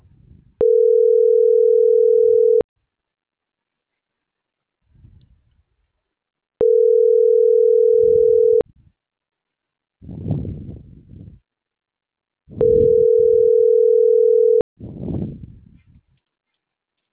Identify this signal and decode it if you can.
Ringback tone (North America), 440 + 480 Hz.
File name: CENTRAL MAERSK / out gabrielle.melendez - .25827